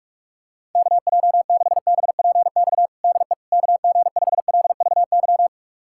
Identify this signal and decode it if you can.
Morse code, heard as a keyed tone, about 41 wpm, 700 Hz. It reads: KJ6BPX DE KG5LVY
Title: USA Callsign DE Callsign - 6 Characters - 45wpm